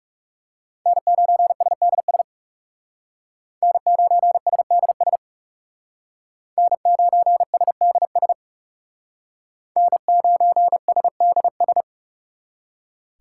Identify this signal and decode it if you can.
Morse code, heard as a keyed tone, about 40 wpm, 700 Hz. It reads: N9SDS N9SDS N9SDS N9SDS